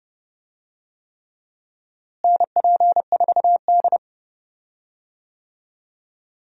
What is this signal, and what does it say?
Morse code, heard as a keyed tone, about 32 wpm, 700 Hz. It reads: NP4D
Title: USA Call Signs - 4 Characters - Repeated 3x - 30wpm